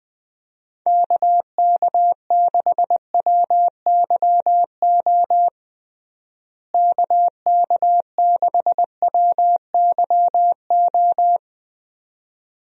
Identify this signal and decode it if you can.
Morse code, heard as a keyed tone, about 21 wpm, 700 Hz. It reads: KK6WYO KK6WYO
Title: USA Call Signs - 6 Characters - Repeated 2x - 20wpm